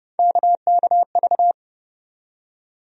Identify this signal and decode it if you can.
Morse code, heard as a keyed tone, about 30 wpm, 700 Hz. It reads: KKV
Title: USA Call Sign Postfixes - 3 Letters 30wpm